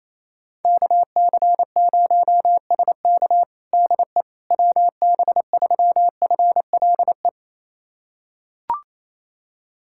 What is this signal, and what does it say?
Morse code, heard as a keyed tone, about 28 wpm, 700 Hz. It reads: KC0SK DE WB3FLE
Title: USA Callsign DE Callsign 28wpm